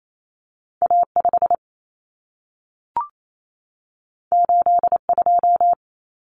Morse code, heard as a keyed tone, about 28 wpm, 700 Hz. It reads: A5 82
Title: Common Cut Numbers - 2-Digit 28wpm